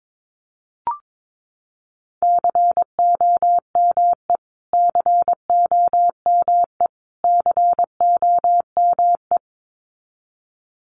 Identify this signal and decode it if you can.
Morse code, heard as a keyed tone, about 22 wpm, 700 Hz. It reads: COME COME COME
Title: Top 300 Words Repeated 22wpm